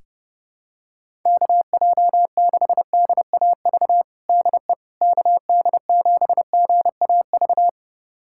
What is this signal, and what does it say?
Morse code, heard as a keyed tone, about 30 wpm, 700 Hz. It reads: KJ6DAV DE KD7GAV
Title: USA Callsign DE Callsign - 6 Characters - 30wpm